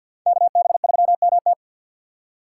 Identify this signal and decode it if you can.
Morse code, heard as a keyed tone, about 51 wpm, 700 Hz. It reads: KB3MT
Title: USA Call Signs - 5 Characters - 2x Character Spacing - 50wpm